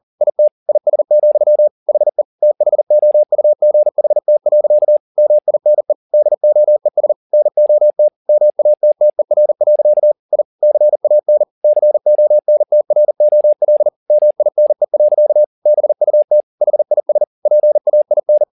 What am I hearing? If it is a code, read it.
Morse code, heard as a keyed tone, about 40 wpm, 600 Hz. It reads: IT IS, HE THOUGHT. MINE DOES NOT MATTER. I CAN CONTROL MINE. BUT HIS PAIN
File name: the old man and the sea 2 40wpm